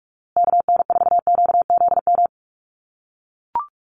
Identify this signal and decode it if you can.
Morse code, heard as a keyed tone, about 44 wpm, 690 Hz. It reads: KN4QZM
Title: USA Call Signs Repeated 3x 45wpm